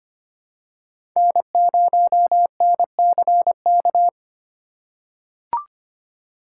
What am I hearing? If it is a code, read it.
Morse code, heard as a keyed tone, about 25 wpm, 700 Hz. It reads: N0NCK
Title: USA Call Signs - 8x Character Spacing - 25wpm